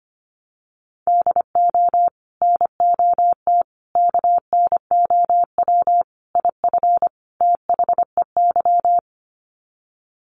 Morse code, heard as a keyed tone, about 25 wpm, 700 Hz. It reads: DO NOT KNOW IF THEY
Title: Sets of 5 Words Encore 25wpm